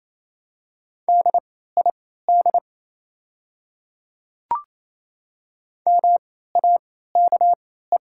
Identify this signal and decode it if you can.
Morse code, heard as a keyed tone, about 28 wpm, spaced as for 16 wpm, 690 Hz. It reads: DID MAKE
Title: 3x Character Spacing - Top 200 Words - 28wpm